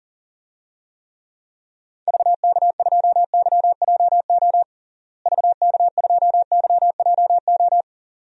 Morse code, heard as a keyed tone, about 40 wpm, 700 Hz. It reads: VK2YJO VK2YJO
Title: DXCC-100 Call Signs - 6 Characters - Repeated 2x - 40wpm